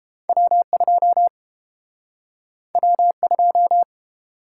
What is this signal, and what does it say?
Morse code, heard as a keyed tone, about 30 wpm, 700 Hz. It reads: W2 W2